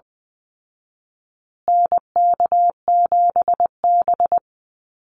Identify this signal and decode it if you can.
Morse code, heard as a keyed tone, about 20 wpm, 700 Hz. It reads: NK7B
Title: Call Signs Repeated 2x - 20wpm